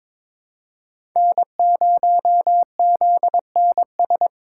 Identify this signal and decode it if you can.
Morse code, heard as a keyed tone, about 21 wpm, 700 Hz. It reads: N0ZNS
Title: USA Call Signs - 5 Characters - 4x Character Spacing - 22wpm